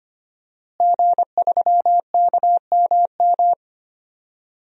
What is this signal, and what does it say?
Morse code, heard as a keyed tone, about 25 wpm, 700 Hz. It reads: G3KMM
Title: DXCC-100 Call Signs - 5 Characters - Repeated 3x - 25wpm